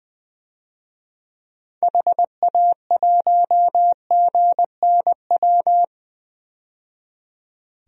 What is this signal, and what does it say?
Morse code, heard as a keyed tone, about 20 wpm, 700 Hz. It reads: HA1GNW
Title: DXCC-100 Call Signs - 6 Characters - 20wpm